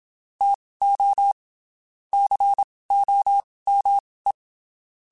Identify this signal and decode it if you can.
Morse code, heard as a keyed tone, about 27 wpm, spaced as for 19 wpm, 800 Hz. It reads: TO COME
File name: (spaced) / QSOs Spaced-mp3s / WA7QOX 1 @25 sp